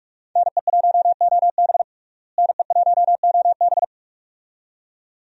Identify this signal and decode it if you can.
Morse code, heard as a keyed tone, about 43 wpm, 700 Hz. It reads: NE1OB NE1OB
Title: USA Call Signs - 5 Characters - Repeated 2x - 45wpm